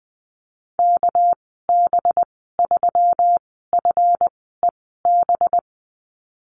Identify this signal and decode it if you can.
Morse code, heard as a keyed tone, about 19 wpm, spaced as for 14 wpm, 700 Hz. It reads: KB3FEB
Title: 231 Lesson 37g - Letter x - Call Signs 20wpm